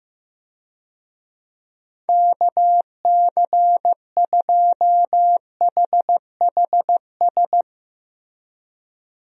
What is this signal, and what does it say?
Morse code, heard as a keyed tone, about 15 wpm, 700 Hz. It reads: KC2HHS
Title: USA Call Signs - 6 Characters - Repeated 2x - 15wpm